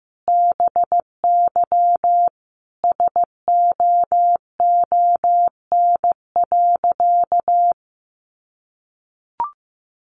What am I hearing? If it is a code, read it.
Morse code, heard as a keyed tone, about 15 wpm, 700 Hz. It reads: BY SOON.
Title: Sentences from Top 1500 Words 15wpm